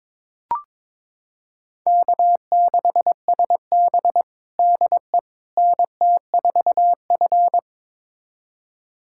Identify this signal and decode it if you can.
Morse code, heard as a keyed tone, about 22 wpm, 700 Hz. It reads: K6SB DE NT4F
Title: USA Callsign DE Callsign - 4 Characters - 22wpm